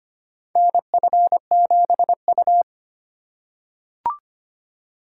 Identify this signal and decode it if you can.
Morse code, heard as a keyed tone, about 25 wpm, 700 Hz. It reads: NF7U
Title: USA Call Signs - 4 Characters - 3x Character Spacing - 25wpm